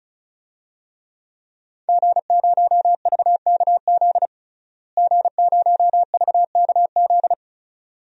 Morse code, heard as a keyed tone, about 34 wpm, 700 Hz. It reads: G0VKZ G0VKZ
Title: DXCC-100 Call Signs - 5 Characters - Repeated 2x - 35wpm